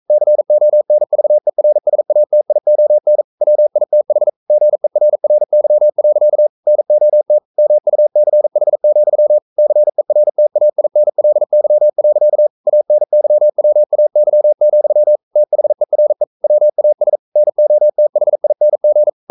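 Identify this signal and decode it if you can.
Morse code, heard as a keyed tone, about 42 wpm, 600 Hz. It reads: CONVERSATION WITH GERRY. NOT MUCH, CERTAINLY. ANYWAY, THERE WAS NOTHING
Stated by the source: Morse code sending of an English novel at 42 wpm.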